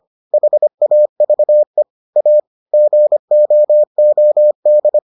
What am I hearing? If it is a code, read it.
Morse code, heard as a keyed tone, about 25 wpm, 600 Hz. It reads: HAVE A GOOD